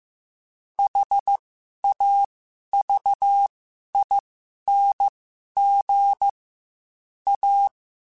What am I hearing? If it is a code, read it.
Morse code, heard as a keyed tone, about 15 wpm, spaced as for 11 wpm, 800 Hz. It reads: HAVING A
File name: Quotes / Quotes-mp3s / ChildAsleep @13